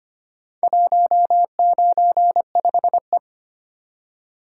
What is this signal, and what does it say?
Morse code, heard as a keyed tone, about 25 wpm, 700 Hz. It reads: 195E
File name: practice / 25wpm / Four Letter-Number Combinations - 25wpm